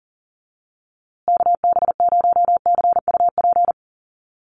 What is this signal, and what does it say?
Morse code, heard as a keyed tone, about 40 wpm, 700 Hz. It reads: KB0CUP
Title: USA Call Signs Repeated 3x 40wpm